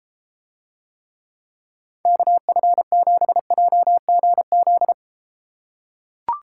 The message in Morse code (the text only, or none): KF7JGZ